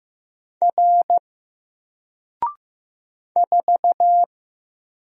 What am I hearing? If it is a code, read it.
Morse code, heard as a keyed tone, about 15 wpm, 700 Hz. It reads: R 4
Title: Single Character (Letter:Number:Punctuation:Prosign) 15wpm